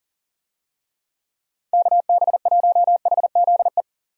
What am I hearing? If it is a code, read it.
Morse code, heard as a keyed tone, about 37 wpm, 700 Hz. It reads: KB1HZE
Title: USA Call Signs - 6 Characters - 40wpm